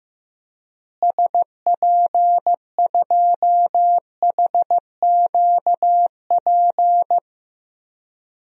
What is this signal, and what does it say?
Morse code, heard as a keyed tone, about 15 wpm, 700 Hz. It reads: SP2HQP
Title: DXCC-100 Call Signs - 6 Characters - Repeated 3x - 15wpm